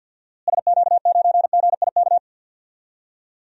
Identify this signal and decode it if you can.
Morse code, heard as a keyed tone, about 48 wpm, 700 Hz. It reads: SQ9GIK
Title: DXCC-100 Call Signs - 6 Characters - Repeated 3x - 50wpm